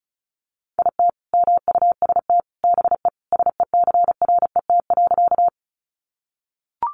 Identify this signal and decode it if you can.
Morse code, heard as a keyed tone, about 34 wpm, 700 Hz. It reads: IT MUST BE SECRET.
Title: Sentences from Top 1200 Words 35wpm